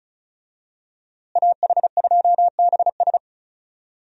Morse code, heard as a keyed tone, about 34 wpm, 700 Hz. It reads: AH2BS
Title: USA Call Signs - 5 Characters - Repeated 3x - 35wpm